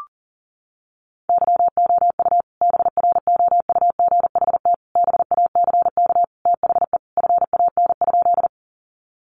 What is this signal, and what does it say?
Morse code, heard as a keyed tone, about 40 wpm, 700 Hz. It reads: YOU BROUGHT BACK THE FAN?
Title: Sentences from Top 1300 Words 40wpm